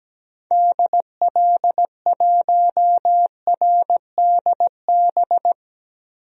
Morse code, heard as a keyed tone, about 17 wpm, 700 Hz. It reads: DL1RDB
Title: DXCC-100 Call Signs - 6 Characters - Repeated 3x - 17wpm